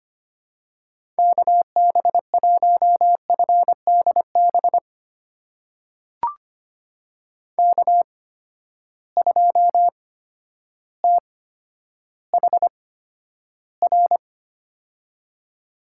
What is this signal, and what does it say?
Morse code, heard as a keyed tone, about 25 wpm, spaced as for 11 wpm, 700 Hz. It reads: KB1FDB K2THR
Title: USA Call Signs - 8x Character Spacing - 25wpm